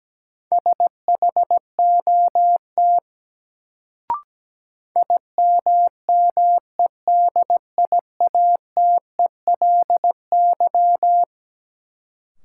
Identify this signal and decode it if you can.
Morse code, heard as a keyed tone, about 17 wpm, 700 Hz. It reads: SHOT IMMEDIATELY
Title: Sentences from Top 1100 Words - New Word Drill - 17wpm